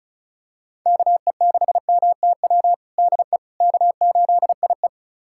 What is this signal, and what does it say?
Morse code, heard as a keyed tone, about 36 wpm, 700 Hz. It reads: KE6MTW DE K8IE
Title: USA Callsign DE Callsign 35wpm